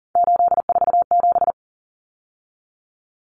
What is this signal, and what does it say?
Morse code, heard as a keyed tone, about 40 wpm, 700 Hz. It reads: 847
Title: Numbers - 3-Digit 40wpm